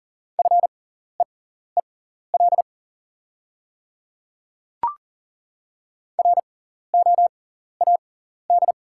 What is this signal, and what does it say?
Morse code, heard as a keyed tone, about 40 wpm, spaced as for 14 wpm, 690 Hz. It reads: FEEL ROAD
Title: 6x Character Spacing - Top 500 Words - 40wpm